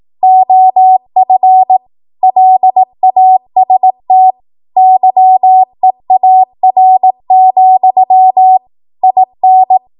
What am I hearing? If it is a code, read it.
Morse code, heard as a keyed tone, about 18 wpm, 750 Hz. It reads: OF LAST YEAR, IN
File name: Archive / 18 WPM / 18WPM